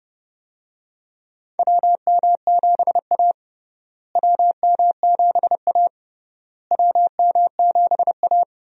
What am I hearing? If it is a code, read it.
Morse code, heard as a keyed tone, about 29 wpm, 700 Hz. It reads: WM7A WM7A WM7A
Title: USA Call Signs - 4 Characters - Repeated 3x - 30wpm